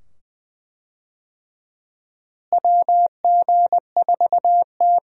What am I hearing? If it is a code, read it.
Morse code, heard as a keyed tone, about 21 wpm, 700 Hz. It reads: WG4T
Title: USA Call Signs - 4 Characters - Repeated 2x - 20wpm